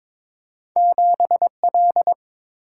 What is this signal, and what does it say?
Morse code, heard as a keyed tone, about 22 wpm, 700 Hz. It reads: 7L
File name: practice / 22wpm / Two Letter-Number Combinations - 22wpm